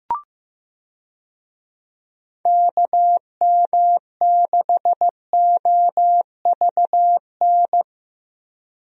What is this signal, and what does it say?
Morse code, heard as a keyed tone, about 15 wpm, 700 Hz. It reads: KM6OVN
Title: USA Call Signs - 6 Characters - Repeated 2x - 15wpm